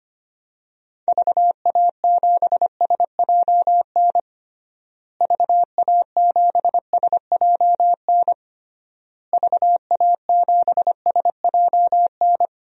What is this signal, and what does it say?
Morse code, heard as a keyed tone, about 25 wpm, 700 Hz. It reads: VA7SJN VA7SJN VA7SJN
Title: DXCC-100 Call Signs - 6 Characters - Repeated 3x - 25wpm